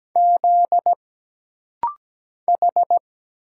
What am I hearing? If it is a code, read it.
Morse code, heard as a keyed tone, about 17 wpm, 700 Hz. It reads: Z H
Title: Single Letters - Rapid-Fire 17wpm